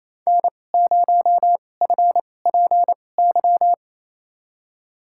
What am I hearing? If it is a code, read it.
Morse code, heard as a keyed tone, about 28 wpm, spaced as for 20 wpm, 700 Hz. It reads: N0FPY
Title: USA Call Signs - 2x Character Spacing - 28wpm